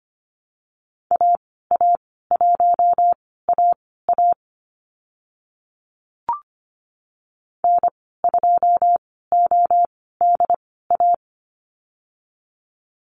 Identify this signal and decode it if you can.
Morse code, heard as a keyed tone, about 25 wpm, spaced as for 16 wpm, 700 Hz. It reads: AA1AA N2ODA
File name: Lesson 15g - Number 5 - Call Signs 25wpm